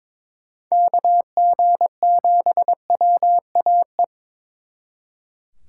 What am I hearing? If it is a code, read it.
Morse code, heard as a keyed tone, about 26 wpm, 700 Hz. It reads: KG7WAE
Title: Speed Racing - USA 2x3 Call Signs 22wpm